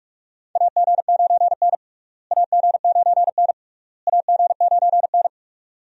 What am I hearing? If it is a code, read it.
Morse code, heard as a keyed tone, about 46 wpm, 700 Hz. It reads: AG9N AG9N AG9N
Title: USA Call Signs - 4 Characters - Repeated 3x - 45wpm